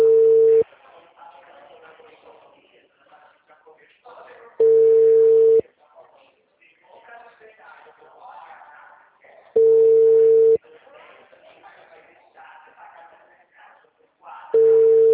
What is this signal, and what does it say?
Ringback tone (China, 450 Hz), with speech.
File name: out- -2010- - - .85326